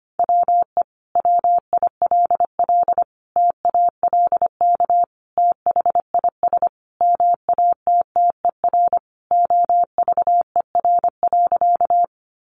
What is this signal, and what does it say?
Morse code, heard as a keyed tone, about 25 wpm, 700 Hz. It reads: WE WILL TALK THIS MATTER OVER.
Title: Sentences from Top 700 Words 25wpm